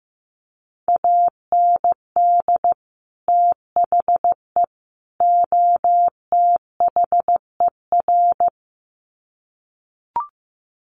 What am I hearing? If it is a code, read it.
Morse code, heard as a keyed tone, about 15 wpm, 700 Hz. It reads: AND THE OTHER
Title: Sets of 3 Words 15wpm